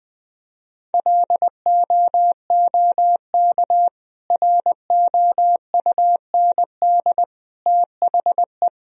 Morse code, heard as a keyed tone, about 20 wpm, 700 Hz. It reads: LOOK ROUND THE
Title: Sentences from Top 1000 Words - Review Part 1 - 20wpm